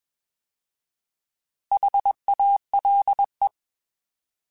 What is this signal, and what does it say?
Morse code, heard as a keyed tone, about 21 wpm, 800 Hz. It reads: HALE
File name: Alphabetical / Alphabetical mp3s / Hs @20